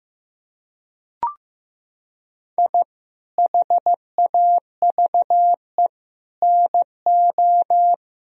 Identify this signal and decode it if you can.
Morse code, heard as a keyed tone, about 15 wpm, 700 Hz. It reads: I HAVE NO
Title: Sentences from Top 1000 Words - Review Part 1 - 15wpm